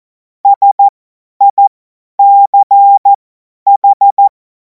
Morse code, heard as a keyed tone, about 14 wpm, spaced as for 10 wpm, 800 Hz. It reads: SICH
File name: Morsekurs / Witze / Witz 10. Februar 2026 50 F70